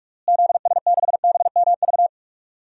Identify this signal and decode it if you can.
Morse code, heard as a keyed tone, about 46 wpm, 700 Hz. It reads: ZS6BMV
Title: DXCC-100 Call Signs - 6 Characters - Repeated 3x - 45wpm